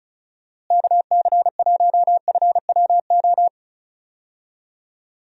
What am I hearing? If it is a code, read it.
Morse code, heard as a keyed tone, about 36 wpm, 700 Hz. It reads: KC1FWO
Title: USA Call Signs - 6 Characters - Repeated 3x - 35wpm